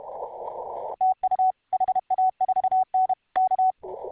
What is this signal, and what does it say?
Morse code, heard as a keyed tone, about 32 wpm, 750 Hz. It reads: TU HA4N K